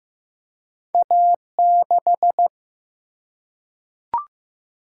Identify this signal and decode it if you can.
Morse code, heard as a keyed tone, about 15 wpm, 700 Hz. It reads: A6
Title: Single-Letter Single-Number 15wpm